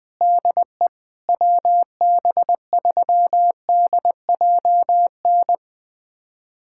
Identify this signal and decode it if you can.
Morse code, heard as a keyed tone, about 20 wpm, 700 Hz. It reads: DE WB3DJN
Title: USA Callsign DE Callsign 20wpm